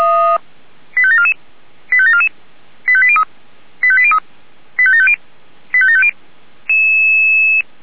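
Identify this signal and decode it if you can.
Five-tone selective call (ZVEI), heard as quick runs of five tones, tones of about 70 ms. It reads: 85744 85744 85803 85803 85750 85750
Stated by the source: Received on an SDR.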